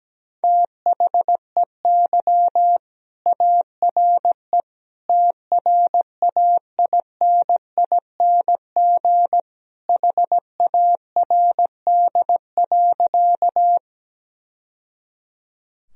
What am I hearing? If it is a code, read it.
Morse code, heard as a keyed tone, about 17 wpm, 700 Hz. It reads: THEY ARE TRAINING HARD.
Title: Sentences from Top 1000 Words - Review Part 4 - 17wpm